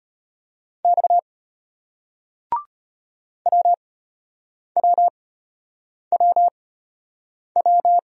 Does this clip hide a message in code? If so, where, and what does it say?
Yes: Morse code from 0.8 to 8.0 s, heard as a keyed tone, about 34 wpm, spaced as for 8 wpm, 700 Hz. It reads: X WWWW